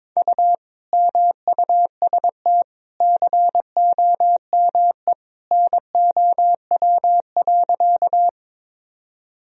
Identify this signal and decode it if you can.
Morse code, heard as a keyed tone, about 22 wpm, 700 Hz. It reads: U MUST COME NOW.
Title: Sentences from Top 200 Words and Abbreviations 22wpm